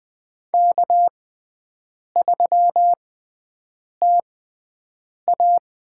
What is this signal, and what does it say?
Morse code, heard as a keyed tone, about 20 wpm, spaced as for 7 wpm, 700 Hz. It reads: K3TA
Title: USA Call Signs - 5 Characters - 6x Character Spacing - 20wpm